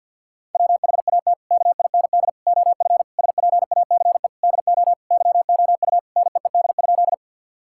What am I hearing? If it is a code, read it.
Morse code, heard as a keyed tone, about 53 wpm, 690 Hz. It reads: WHAT KIND OF SPACE DO YOU NEED?
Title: Sentences from Top 1000 Words - Review Part 5 - 50wpm